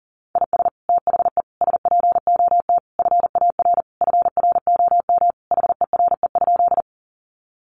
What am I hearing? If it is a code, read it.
Morse code, heard as a keyed tone, about 40 wpm, 700 Hz. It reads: IS THE SPOT FAR FROM HERE?
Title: Sentences from Top 1100 Words 40wpm